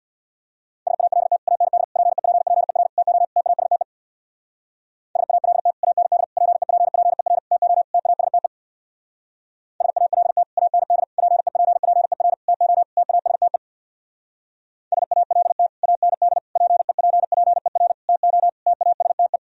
Morse code, heard as a keyed tone, about 64 wpm, 700 Hz. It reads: SALT AND PEPPER TO TASTE SALT AND PEPPER TO TASTE SALT AND PEPPER TO TASTE SALT AND PEPPER TO TASTE